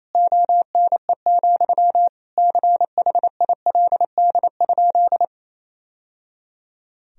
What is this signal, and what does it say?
Morse code, heard as a keyed tone, about 28 wpm, 700 Hz. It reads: ONE, CHILD?
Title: Sentences from Top 1000 Words - Review Part 1 - 28wpm